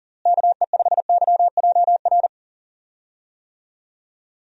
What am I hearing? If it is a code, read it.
Morse code, heard as a keyed tone, about 40 wpm, 700 Hz. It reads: KE5YJR
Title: USA Call Signs - 6 Characters - Repeated 3x - 40wpm